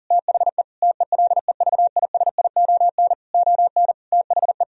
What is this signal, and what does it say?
Morse code, heard as a keyed tone, about 40 wpm, 700 Hz. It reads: THE TELEVISION ON THE
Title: Sentences from Top 1000 Words - Review Part 4 - 40wpm